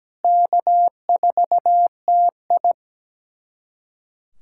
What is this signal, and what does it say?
Morse code, heard as a keyed tone, about 17 wpm, 700 Hz. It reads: K4TI
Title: USA Callsign DE Callsign 17wpm